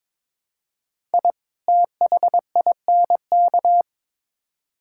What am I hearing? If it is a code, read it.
Morse code, heard as a keyed tone, about 22 wpm, 700 Hz. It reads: I THINK
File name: Build A Phrase - 2-Word Phrases - 22wpm